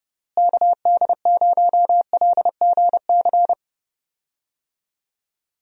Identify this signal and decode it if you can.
Morse code, heard as a keyed tone, about 30 wpm, 700 Hz. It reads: KD0LGC
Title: USA Call Signs - 6 Characters - Repeated 2x - 30wpm